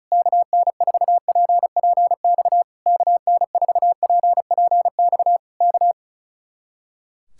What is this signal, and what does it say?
Morse code, heard as a keyed tone, about 32 wpm, 700 Hz. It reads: KN4PPX KN4PPX K
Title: Speed-Racing Common QSO Phrases - 35wpm